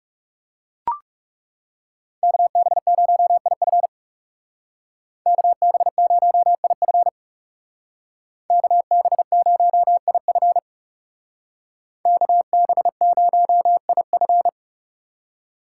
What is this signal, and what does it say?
Morse code, heard as a keyed tone, about 34 wpm, 700 Hz. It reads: KB0IF KB0IF KB0IF KB0IF
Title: Speed Racing - USA 2x2 Call Signs 30wpm